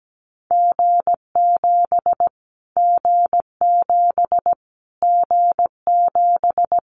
Morse code, heard as a keyed tone, about 17 wpm, 700 Hz. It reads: G7 G7 G7
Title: Callsign Prefixes for Top 25 DXCC - Repeated 3x - 17wpm